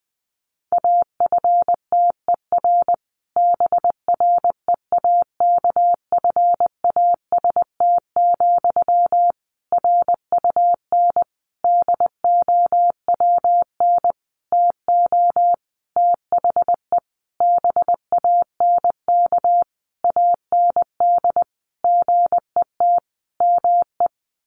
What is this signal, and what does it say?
Morse code, heard as a keyed tone, about 19 wpm, 700 Hz. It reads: AFTER BREAKFAST, RUN DOWN TO THE BANK AND GET ME
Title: Sentences from Top 1500 Words 20wpm